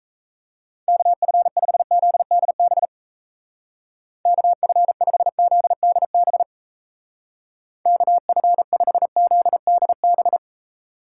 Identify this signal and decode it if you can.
Morse code, heard as a keyed tone, about 33 wpm, 700 Hz. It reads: KF5ZDB KF5ZDB KF5ZDB
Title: Speed Racing - USA 2x3 Call Signs 28wpm